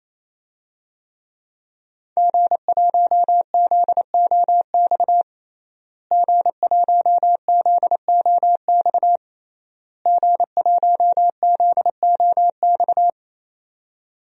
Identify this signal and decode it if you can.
Morse code, heard as a keyed tone, about 28 wpm, 700 Hz. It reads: G1ZOX G1ZOX G1ZOX
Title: DXCC-100 Call Signs - 5 Characters - Repeated 3x - 28wpm